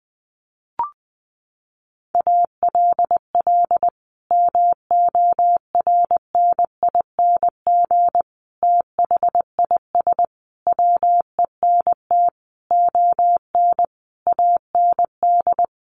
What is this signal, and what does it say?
Morse code, heard as a keyed tone, about 19 wpm, 700 Hz. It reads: ALL MORNING THIS WENT ON AND
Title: Sentences from Top 1500 Words 20wpm